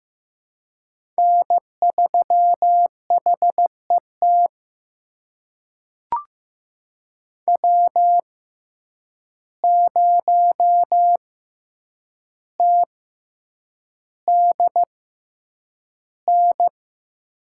Morse code, heard as a keyed tone, about 15 wpm, spaced as for 8 wpm, 700 Hz. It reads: N3HET W0TDN
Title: USA Call Signs - 5 Characters - 6x Character Spacing - 15wpm